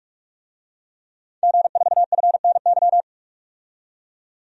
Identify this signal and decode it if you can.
Morse code, heard as a keyed tone, about 46 wpm, 700 Hz. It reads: G4FNY
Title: DXCC-100 Call Signs - 5 Characters - 45wpm